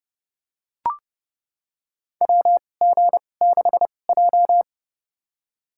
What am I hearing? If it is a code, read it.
Morse code, heard as a keyed tone, about 30 wpm, spaced as for 22 wpm, 700 Hz. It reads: WG6J
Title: USA Call Signs - 4 Characters - 2x Character Spacing - 30wpm